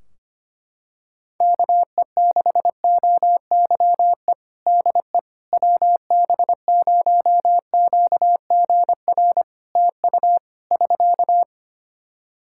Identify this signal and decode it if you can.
Morse code, heard as a keyed tone, about 25 wpm, 700 Hz. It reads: KE6OYE DE WB0QGR TU <SK>
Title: Practice QSO - Exchange 4 - Ending - 25wpm